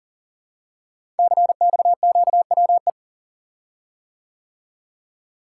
Morse code, heard as a keyed tone, about 39 wpm, 690 Hz. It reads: CXQWE
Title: Code Groups - 5 Letters - 40wpm